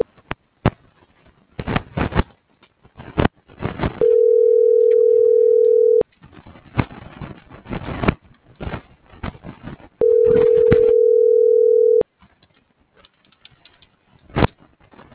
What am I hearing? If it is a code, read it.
Ringback tone (North America), 440 + 480 Hz.
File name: CENTRAL GENERAL / out tania.lemus - .433